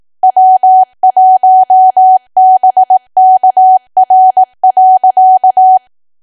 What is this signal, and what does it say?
Morse code, heard as a keyed tone, about 18 wpm, 750 Hz. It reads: W1BKR.
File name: Archive / 18 WPM / 18WPM